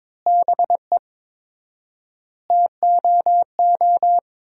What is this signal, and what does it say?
Morse code, heard as a keyed tone, about 23 wpm, 700 Hz. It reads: BE TOO
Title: 4x Word-Spacing - Sentences from Top 300 Words - 22wpm